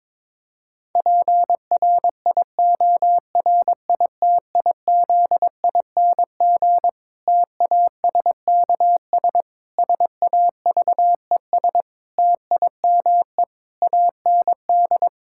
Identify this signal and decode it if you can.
Morse code, heard as a keyed tone, about 22 wpm, 700 Hz. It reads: PRIORITIZING TASKS SAVES TIME AND
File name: Difficult Sentences with Hints - 22wpm